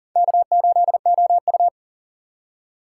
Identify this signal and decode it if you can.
Morse code, heard as a keyed tone, about 37 wpm, 700 Hz. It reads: K8OU
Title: USA Call Signs - 4 Characters - 3x Character Spacing - 40wpm